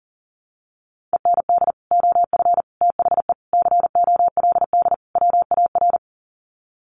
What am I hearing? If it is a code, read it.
Morse code, heard as a keyed tone, about 40 wpm, 700 Hz. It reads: END OF THE COLD WAR
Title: Sets of 5 Words Encore 40wpm